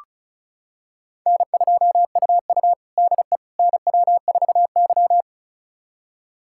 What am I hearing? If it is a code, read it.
Morse code, heard as a keyed tone, about 34 wpm, 700 Hz. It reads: N2UU DE NW4Y
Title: USA Callsign DE Callsign - 4 Characters - 35wpm